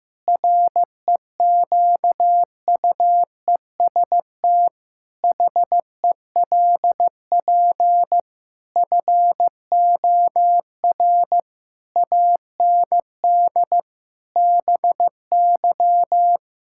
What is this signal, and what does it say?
Morse code, heard as a keyed tone, about 15 wpm, 700 Hz. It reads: REQUEST HELP FOR AND BY
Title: Pangram Challenge v2 - 15wpm